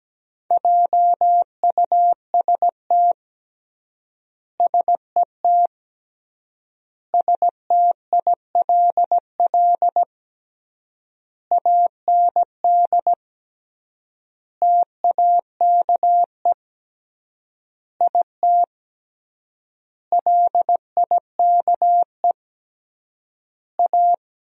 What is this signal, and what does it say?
Morse code, heard as a keyed tone, about 17 wpm, 700 Hz. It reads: JUST SET STILL AND TAKE IT LIKE A
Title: 3x Word-Spacing - Sentences from Top 300 Words - 17wpm